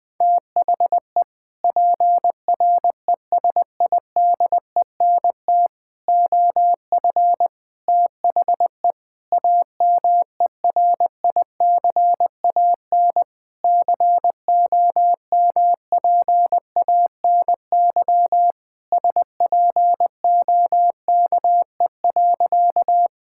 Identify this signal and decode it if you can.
Morse code, heard as a keyed tone, about 20 wpm, 700 Hz. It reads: THE PRESIDENT OF THE AMERICAN COMPANY SPOKE.
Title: Sentences with 3-Word Phrases - Spoken Beginning - 20wpm